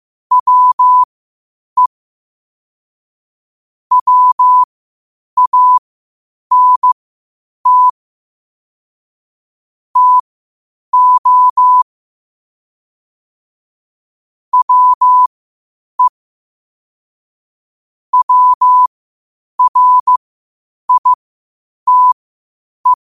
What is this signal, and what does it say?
Morse code, heard as a keyed tone, about 15 wpm, spaced as for 8 wpm, 1000 Hz. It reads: WE WANT TO WE WRITE
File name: qotdfiles / 8wpm